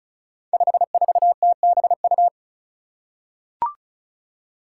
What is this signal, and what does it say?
Morse code, heard as a keyed tone, about 36 wpm, 700 Hz. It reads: 54TBU
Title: Five Challenging Letter-Number Combinations 35wpm